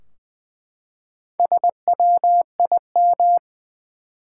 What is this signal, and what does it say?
Morse code, heard as a keyed tone, about 20 wpm, 700 Hz. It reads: SWIM
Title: Four Letter Words - 20wpm